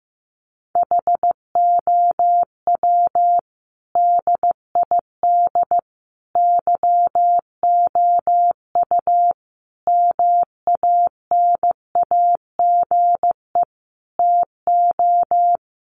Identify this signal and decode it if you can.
Morse code, heard as a keyed tone, about 15 wpm, 700 Hz. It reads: HOW DID YOU MANAGE TO
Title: Sentences from Top 1400 Words 15wpm